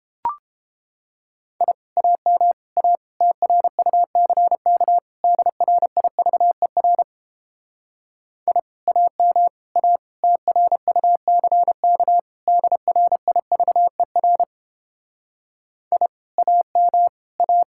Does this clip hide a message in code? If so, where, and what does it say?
Yes: Morse code from 1.6 to 17.6 s, heard as a keyed tone, about 29 wpm, 700 Hz. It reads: I AM A TRUCK DRIVER I AM A TRUCK DRIVER I AM A